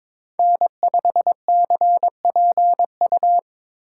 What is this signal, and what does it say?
Morse code, heard as a keyed tone, about 22 wpm, 700 Hz. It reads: N5CPU
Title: USA Call Signs - 5 Characters - 3x Character Spacing - 22wpm